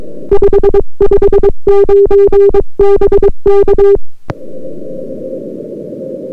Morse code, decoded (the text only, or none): 559 BK